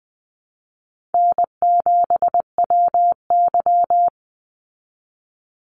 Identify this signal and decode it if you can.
Morse code, heard as a keyed tone, about 20 wpm, 700 Hz. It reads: N7WY
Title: Easy Call Signs Repeated 3x - 20wpm